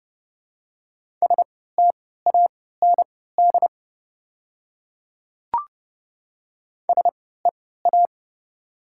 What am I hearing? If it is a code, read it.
Morse code, heard as a keyed tone, about 30 wpm, spaced as for 17 wpm, 700 Hz. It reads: STAND SEA
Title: 3x Character Spacing - Top 300 Words - 30wpm